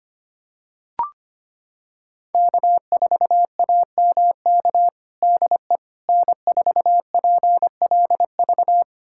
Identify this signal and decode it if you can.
Morse code, heard as a keyed tone, about 25 wpm, 700 Hz. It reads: K4AMK DE N4PLV
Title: USA Callsign DE Callsign - 5 Characters - 25wpm